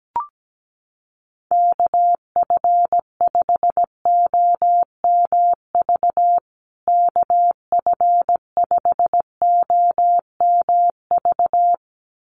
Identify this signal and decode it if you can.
Morse code, heard as a keyed tone, about 17 wpm, 700 Hz. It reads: KF5OMV KF5OMV
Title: USA Call Signs Repeated 2x 17wpm